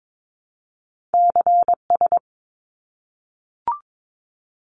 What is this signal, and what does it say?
Morse code, heard as a keyed tone, about 22 wpm, 700 Hz. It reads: CS
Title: Abbreviations 22wpm